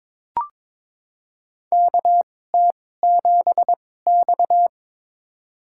Morse code, heard as a keyed tone, about 21 wpm, spaced as for 16 wpm, 700 Hz. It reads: KT7X
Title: USA Call Signs - 4 Characters - 2x Character Spacing - 22wpm